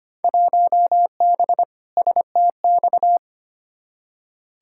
Morse code, heard as a keyed tone, about 25 wpm, 700 Hz. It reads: 1B STX
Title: Field Day Exchanges - 25wpm